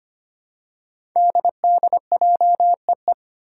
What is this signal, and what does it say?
Morse code, heard as a keyed tone, about 25 wpm, 700 Hz. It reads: DDJEE
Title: Five Challenging Letter-Number Combinations 25wpm